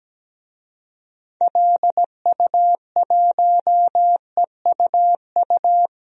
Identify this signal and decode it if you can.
Morse code, heard as a keyed tone, about 17 wpm, 700 Hz. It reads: LU1EUU
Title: DXCC-100 Call Signs - 6 Characters - Repeated 2x - 17wpm